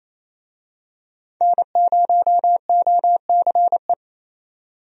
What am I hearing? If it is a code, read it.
Morse code, heard as a keyed tone, about 28 wpm, 700 Hz. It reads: N0OCE
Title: USA Call Signs - 5 Characters - 28wpm